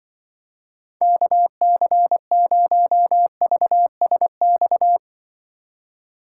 Morse code, heard as a keyed tone, about 25 wpm, 700 Hz. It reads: KC0VSX
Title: Speed Racing - USA 2x3 Call Signs 20wpm